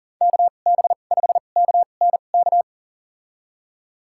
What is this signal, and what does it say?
Morse code, heard as a keyed tone, about 38 wpm, spaced as for 28 wpm, 700 Hz. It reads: KB5KNK
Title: USA Call Signs - 6 Characters - 2x Character Spacing - 40wpm